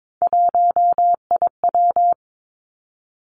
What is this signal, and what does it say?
Morse code, heard as a keyed tone, about 22 wpm, 700 Hz. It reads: 1IW
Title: Three Challenging Letter-Number Combinations 22wpm